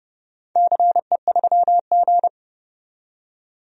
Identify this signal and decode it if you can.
Morse code, heard as a keyed tone, about 30 wpm, 700 Hz. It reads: CE3G
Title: DXCC-100 Call Signs - 4 Characters - 30wpm